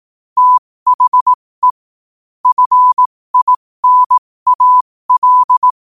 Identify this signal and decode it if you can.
Morse code, heard as a keyed tone, about 18 wpm, spaced as for 15 wpm, 1000 Hz. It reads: THE FINAL
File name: qotdfiles / 15wpm